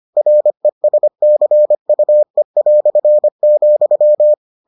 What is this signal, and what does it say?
Morse code, heard as a keyed tone, about 25 wpm, 600 Hz. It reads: RESCUE",